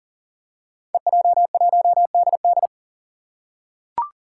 Morse code, heard as a keyed tone, about 38 wpm, 700 Hz. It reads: EJ1DD
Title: Five Challenging Letter-Number Combinations 40wpm